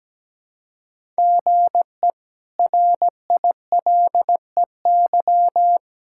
Morse code, heard as a keyed tone, about 17 wpm, 700 Hz. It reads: GE RILEY
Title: K1USN SST QSO Practice - 17wpm